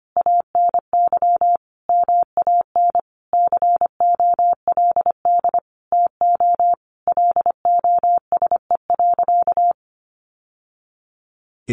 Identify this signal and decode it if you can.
Morse code, heard as a keyed tone, about 25 wpm, 700 Hz. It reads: ANY MAN COLD TO LOSE.
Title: Sentences from Top 1100 Words 25wpm